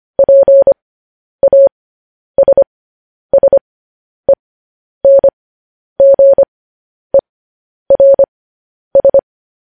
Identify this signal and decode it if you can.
Morse code, heard as a keyed tone, about 25 wpm, spaced as for 10 wpm, 550 Hz. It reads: PASSENGERS
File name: podcast / News headlines-1x-25x10